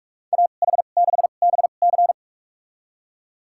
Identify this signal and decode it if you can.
Morse code, heard as a keyed tone, about 41 wpm, spaced as for 31 wpm, 700 Hz. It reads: AH6BC
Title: USA Call Signs - 5 Characters - 2x Character Spacing - 45wpm